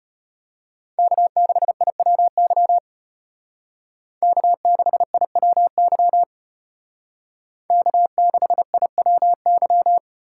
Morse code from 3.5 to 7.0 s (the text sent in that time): K6IWY